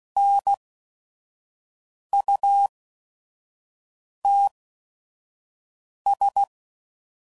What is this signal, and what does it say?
Morse code, heard as a keyed tone, about 16 wpm, spaced as for 5 wpm, 800 Hz. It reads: NUTS
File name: Prefixes / By First Letter / Ns